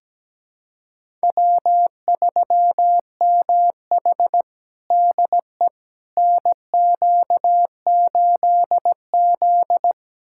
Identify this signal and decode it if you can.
Morse code, heard as a keyed tone, about 17 wpm, 700 Hz. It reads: W3MH DE NQ8Z
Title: USA Callsign DE Callsign - 4 Characters - 17wpm